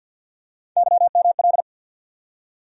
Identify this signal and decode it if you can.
Morse code, heard as a keyed tone, about 48 wpm, 700 Hz. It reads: YML
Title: USA Call Sign Postfixes - 3 Letters 50wpm